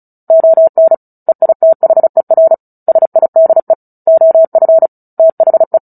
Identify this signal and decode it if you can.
Morse code, heard as a keyed tone, about 34 wpm, 650 Hz. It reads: ON EITHER SIDE OF THE
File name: Book Two / 09-Chapter 9 - 35 WPM